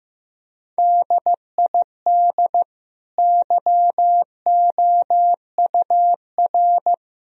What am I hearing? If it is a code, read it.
Morse code, heard as a keyed tone, about 15 wpm, 700 Hz. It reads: DID YOUR
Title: Sentences from Top 1000 Words - Review Part 3 - 15wpm